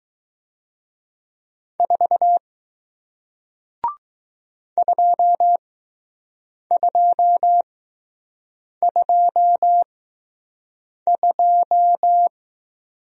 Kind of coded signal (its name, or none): Morse code